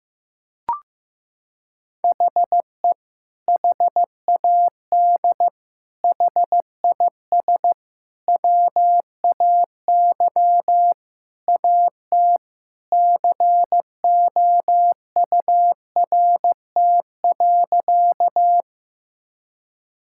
Morse code, heard as a keyed tone, about 16 wpm, 700 Hz. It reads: HE HAD HIS WAY AT COURT.
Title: Sentences from Top 1600 Words 15wpm